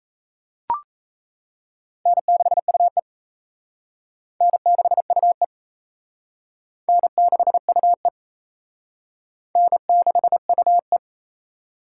Morse code, heard as a keyed tone, about 35 wpm, 700 Hz. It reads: N6UE N6UE N6UE N6UE
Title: Speed Racing - USA 1x2 Call Signs 28wpm